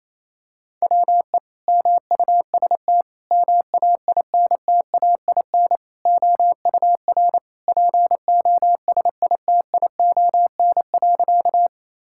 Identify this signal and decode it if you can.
Morse code, heard as a keyed tone, about 28 wpm, 700 Hz. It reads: WE MUST MAINTAIN OUR POSITION.
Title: Sentences from Top 1000 Words - Review Part 5 - 28wpm